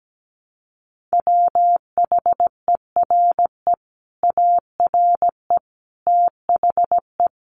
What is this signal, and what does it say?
Morse code, heard as a keyed tone, about 17 wpm, 700 Hz. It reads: WHERE ARE THE
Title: Sentences from Top 1200 Words 17wpm